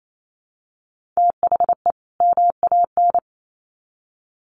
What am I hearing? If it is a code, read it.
Morse code, heard as a keyed tone, about 28 wpm, 700 Hz. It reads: THE MAN
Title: Sets of 2 Words 28wpm